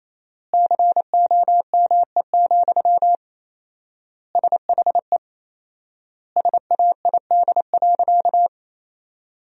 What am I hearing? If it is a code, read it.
Morse code, heard as a keyed tone, about 28 wpm, 700 Hz. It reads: COME, SHE SAID.
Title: 4x Word-Spacing - Sentences from Top 100 Words - 28wpm